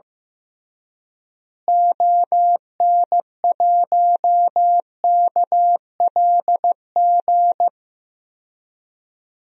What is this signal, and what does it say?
Morse code, heard as a keyed tone, about 16 wpm, 700 Hz. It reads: ON1KLG
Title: DXCC-100 Call Signs - 6 Characters - Repeated 3x - 15wpm